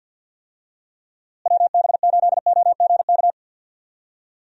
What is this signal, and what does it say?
Morse code, heard as a keyed tone, about 50 wpm, 700 Hz. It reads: WB8OGK
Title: USA Call Signs - 6 Characters - Repeated 3x - 50wpm